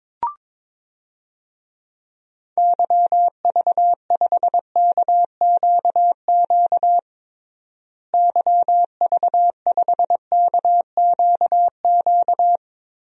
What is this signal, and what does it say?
Morse code, heard as a keyed tone, about 22 wpm, 700 Hz. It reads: YV5KQQ YV5KQQ
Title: DXCC-100 Call Signs - 6 Characters - Repeated 3x - 22wpm